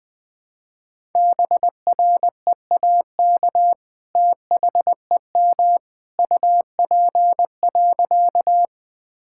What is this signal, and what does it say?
Morse code, heard as a keyed tone, about 21 wpm, 700 Hz. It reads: BREAK THEM UP.
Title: Sentences from Top 1000 Words - Review Part 6 - 20wpm